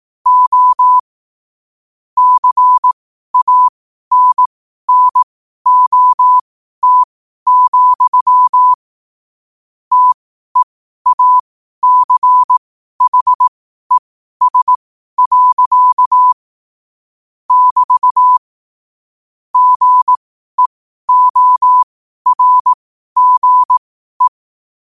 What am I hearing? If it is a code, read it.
Morse code, heard as a keyed tone, about 18 wpm, spaced as for 12 wpm, 1000 Hz. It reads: O CANNOT, TEACHES. <BT> GEORGE
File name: qotdfiles / 12wpm